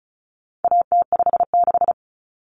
Morse code, heard as a keyed tone, about 34 wpm, 700 Hz. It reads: AT56
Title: Common Cut Numbers - 4-Digit 35wpm